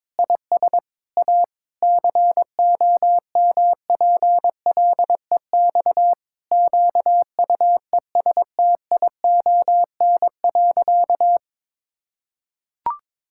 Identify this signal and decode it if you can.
Morse code, heard as a keyed tone, about 21 wpm, 700 Hz. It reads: IS A COMPLEX QUESTION.
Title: Sentences from Top 1000 Words - Review Part 6 - 22wpm